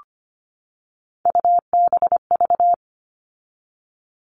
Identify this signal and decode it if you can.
Morse code, heard as a keyed tone, about 25 wpm, 700 Hz. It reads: UBV
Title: All Cut Numbers - 3-Digit 25wpm